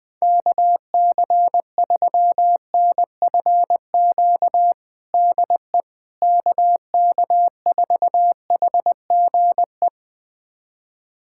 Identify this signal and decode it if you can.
Morse code, heard as a keyed tone, about 20 wpm, 700 Hz. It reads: KC3NFQ DE KK4HGE
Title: USA Callsign DE Callsign - 6 Characters - 20wpm